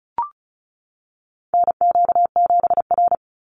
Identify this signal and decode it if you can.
Morse code, heard as a keyed tone, about 34 wpm, 700 Hz. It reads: NQ7R
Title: NA SOTA Chaser Callsigns 35wpm